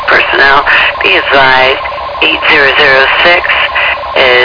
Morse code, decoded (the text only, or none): none